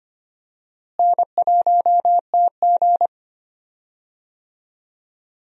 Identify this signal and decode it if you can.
Morse code, heard as a keyed tone, about 25 wpm, 700 Hz. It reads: N1TG
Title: USA Call Signs - 4 Characters - Repeated 3x - 25wpm